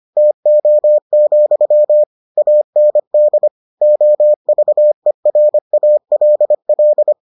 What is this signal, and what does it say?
Morse code, heard as a keyed tone, about 25 wpm, 600 Hz. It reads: TO, AND OVERALL